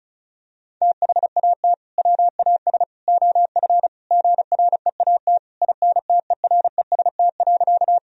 Morse code, heard as a keyed tone, about 36 wpm, 700 Hz. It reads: THAT WAS OF GREAT INTEREST.
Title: Sentences from Top 1000 Words - Review Part 2 - 35wpm